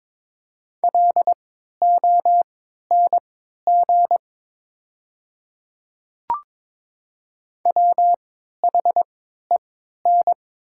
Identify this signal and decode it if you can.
Morse code, heard as a keyed tone, about 22 wpm, spaced as for 12 wpm, 700 Hz. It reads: LONG WHEN
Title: 3x Character Spacing - Top 200 Words - 22wpm